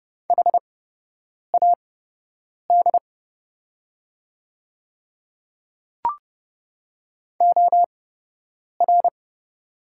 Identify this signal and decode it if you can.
Morse code, heard as a keyed tone, about 30 wpm, spaced as for 8 wpm, 700 Hz. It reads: HAD OR
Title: 8x Character Spacing - Top 100 Words - 30wpm